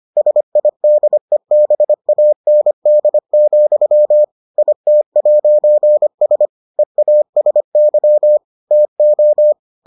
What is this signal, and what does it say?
Morse code, heard as a keyed tone, about 25 wpm, 600 Hz. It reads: SIDEBAND, IT'S EASY TO